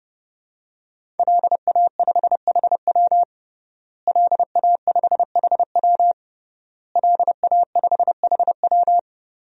Morse code, heard as a keyed tone, about 29 wpm, 700 Hz. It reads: LA5HW LA5HW LA5HW
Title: DXCC-100 Call Signs - 5 Characters - Repeated 3x - 30wpm